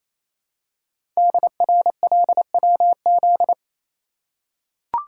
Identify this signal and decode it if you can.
Morse code, heard as a keyed tone, about 27 wpm, 700 Hz. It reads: DRLWZ
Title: Code Groups - 5 Letters - 28wpm